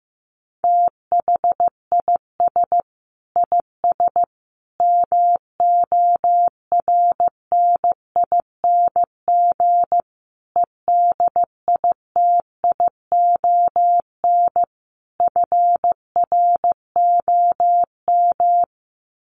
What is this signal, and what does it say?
Morse code, heard as a keyed tone, about 15 wpm, 700 Hz. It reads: THIS IS MORNING EDITION FROM
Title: Sets of 5 Words Encore2 15wpm